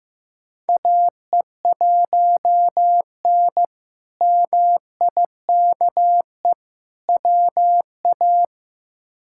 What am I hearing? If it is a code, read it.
Morse code, heard as a keyed tone, about 15 wpm, 700 Hz. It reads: AE1N MIKE WA
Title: K1USN SST QSO Practice - 15wpm